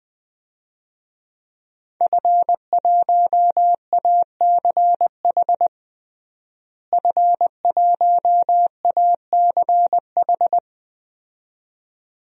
Morse code, heard as a keyed tone, about 21 wpm, 700 Hz. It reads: F1ACH F1ACH
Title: DXCC-100 Call Signs - 5 Characters - Repeated 2x - 20wpm